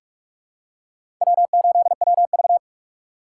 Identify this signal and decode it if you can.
Morse code, heard as a keyed tone, about 46 wpm, 700 Hz. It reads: W8WV
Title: USA Call Signs - 4 Characters - 3x Character Spacing - 45wpm